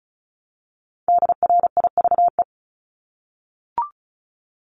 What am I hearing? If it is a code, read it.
Morse code, heard as a keyed tone, about 34 wpm, 700 Hz. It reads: DRIVE
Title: Top 500 Words Repeated 35wpm